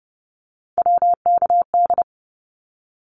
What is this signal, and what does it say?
Morse code, heard as a keyed tone, about 30 wpm, 700 Hz. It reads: WKD
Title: Abbreviations 30wpm